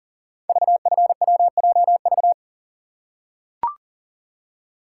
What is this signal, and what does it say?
Morse code, heard as a keyed tone, about 39 wpm, 700 Hz. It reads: VFWJV
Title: Code Groups - 5 Letters - 40wpm